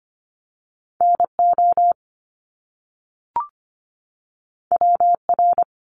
Morse code, heard as a keyed tone, about 25 wpm, 700 Hz. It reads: NO WR
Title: Top Two Letter Combinations in Words 25wpm